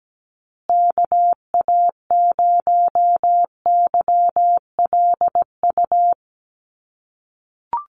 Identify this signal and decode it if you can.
Morse code, heard as a keyed tone, about 17 wpm, 700 Hz. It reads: KA0YLU
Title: USA Call Signs Repeated 2x 17wpm